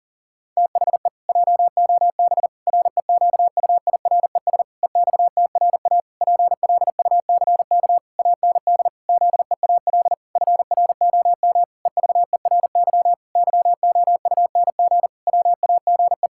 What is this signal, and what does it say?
Morse code, heard as a keyed tone, about 40 wpm, 700 Hz. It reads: THE JOB REQUIRES EXTRA PLUCK AND ZEAL FROM EVERY YOUNG WAGE
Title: Pangram Challenge v2 - 40wpm